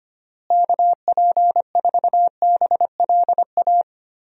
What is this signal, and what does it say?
Morse code, heard as a keyed tone, about 25 wpm, 700 Hz. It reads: KP4BLA
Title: DXCC-100 Call Signs - 6 Characters - Repeated 3x - 25wpm